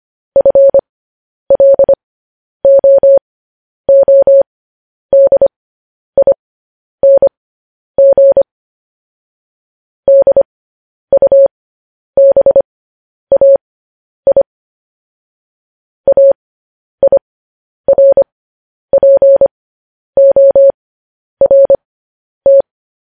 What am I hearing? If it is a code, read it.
Morse code, heard as a keyed tone, about 25 wpm, spaced as for 10 wpm, 550 Hz. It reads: FLOODING DUBAI AIRPORT